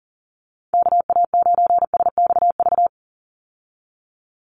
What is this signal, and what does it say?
Morse code, heard as a keyed tone, about 40 wpm, 700 Hz. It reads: KA9SXV
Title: Call Signs Repeated 3x - 40wpm